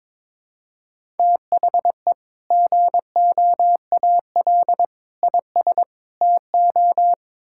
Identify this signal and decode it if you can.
Morse code, heard as a keyed tone, about 22 wpm, 700 Hz. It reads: THE GOAL IS TO
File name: practice / 22wpm / Sentences with 2-Word Phrases - 22wpm